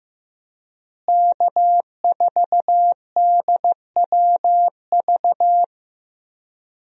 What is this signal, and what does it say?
Morse code, heard as a keyed tone, about 15 wpm, 700 Hz. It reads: K4DWV
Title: USA Call Signs - 5 Characters - Repeated 2x - 15wpm